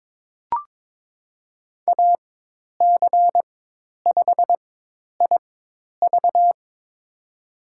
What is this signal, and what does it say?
Morse code, heard as a keyed tone, about 22 wpm, spaced as for 10 wpm, 700 Hz. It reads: AC5IV
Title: USA Call Signs - 5 Characters - 4x Character Spacing - 22wpm